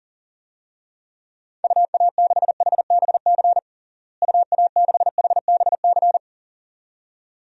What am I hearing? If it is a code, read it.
Morse code, heard as a keyed tone, about 40 wpm, 700 Hz. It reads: UA6HBC UA6HBC
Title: DXCC-100 Call Signs - 6 Characters - Repeated 2x - 40wpm